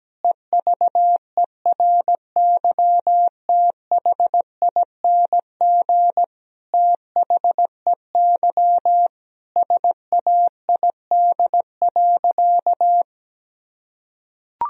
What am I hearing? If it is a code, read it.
Morse code, heard as a keyed tone, about 17 wpm, 700 Hz. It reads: EVERYTHING THEY SAID.
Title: Sentences from Top 1000 Words - Review Part 3 - 17wpm